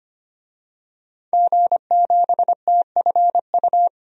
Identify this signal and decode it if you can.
Morse code, heard as a keyed tone, about 25 wpm, 700 Hz. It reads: G7TFU
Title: DXCC-100 Call Signs - 5 Characters - 25wpm